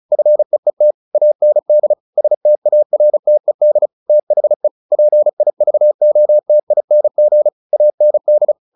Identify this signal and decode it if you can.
Morse code, heard as a keyed tone, about 34 wpm, 600 Hz. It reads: FEET AND STARTED THE PIVOTING AND
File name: the old man and the sea 2 35wpm